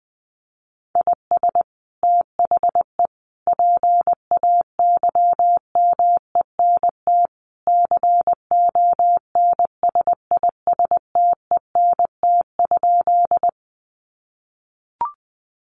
Morse code, heard as a keyed tone, about 20 wpm, 700 Hz. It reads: IS THE PAYMENT CONSISTENT?
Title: Sentences from Top 1500 Words 20wpm